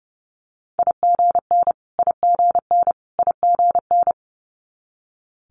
Morse code, heard as a keyed tone, about 30 wpm, 700 Hz. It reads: IGN IGN IGN
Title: Multi-letter Phonemes Repeated 3x 30wpm